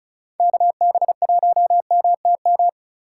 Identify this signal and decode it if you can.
Morse code, heard as a keyed tone, about 34 wpm, 700 Hz. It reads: KB1MTM
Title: USA Call Signs - 6 Characters - 2x Character Spacing - 35wpm